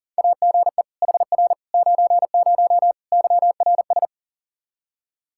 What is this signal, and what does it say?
Morse code, heard as a keyed tone, about 40 wpm, 700 Hz. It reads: AGE HR 90 YRS
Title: Practice QSO - Exchange 3 - More Info - 40wpm